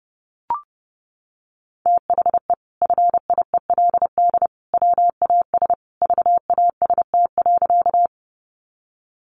Morse code, heard as a keyed tone, about 30 wpm, 700 Hz. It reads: THE FIELD WAS VAST.
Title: Sentences from Top 1300 Words 30wpm